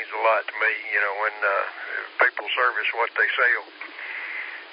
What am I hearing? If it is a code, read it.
Emergency Alert System (SAME) tones, heard as harsh buzzing bursts, with speech. It is heard from 0.5 to 3.7 s.